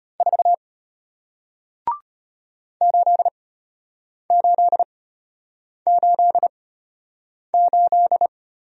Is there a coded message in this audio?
Yes: Morse code from 0.2 to 8.3 s, heard as a keyed tone, about 34 wpm, spaced as for 8 wpm, 700 Hz. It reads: 4 8888